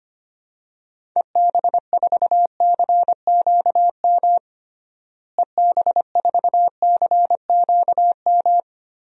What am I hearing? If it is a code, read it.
Morse code, heard as a keyed tone, about 25 wpm, 700 Hz. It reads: EB4CQM EB4CQM
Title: DXCC-100 Call Signs - 6 Characters - Repeated 3x - 25wpm